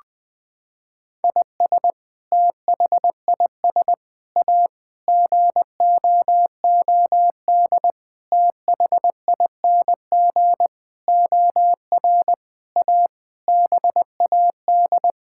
Morse code, heard as a keyed tone, about 20 wpm, 700 Hz. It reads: IS THIS A GOOD THING OR A BAD
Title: Sentences with 7-Word Phrases - 20wpm